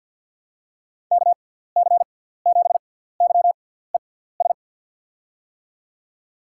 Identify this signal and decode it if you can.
Morse code, heard as a keyed tone, about 47 wpm, spaced as for 17 wpm, 700 Hz. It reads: KC7YES
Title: USA Call Signs - 6 Characters - 6x Character Spacing - 50wpm